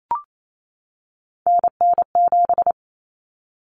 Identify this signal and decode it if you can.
Morse code, heard as a keyed tone, about 28 wpm, 700 Hz. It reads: NN7
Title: Common Cut Numbers - 3-Digit 28wpm